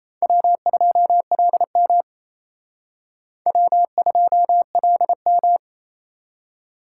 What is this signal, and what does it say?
Morse code, heard as a keyed tone, about 31 wpm, 700 Hz. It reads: W2LM W2LM
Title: Speed Racing - USA 1x2 Call Signs 28wpm